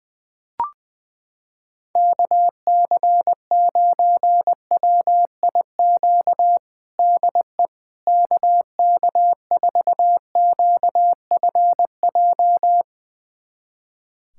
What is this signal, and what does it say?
Morse code, heard as a keyed tone, about 20 wpm, 700 Hz. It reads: KC9WIQ DE KK4QFJ
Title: USA Callsign DE Callsign - 6 Characters - 20wpm